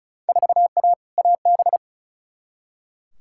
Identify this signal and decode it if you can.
Morse code, heard as a keyed tone, about 34 wpm, 700 Hz. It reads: 4A AB
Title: Field Day Exchanges Encore - 35wpm